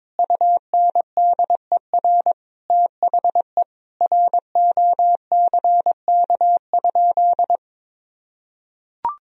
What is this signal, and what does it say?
Morse code, heard as a keyed tone, about 22 wpm, 700 Hz. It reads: UNDER THE ROCK?
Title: Sentences from Top 1000 Words - Review Part 2 - 22wpm